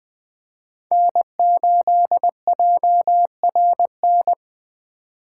Morse code, heard as a keyed tone, about 20 wpm, 700 Hz. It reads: N8JRN
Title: USA Call Signs - 5 Characters - 4x Character Spacing - 20wpm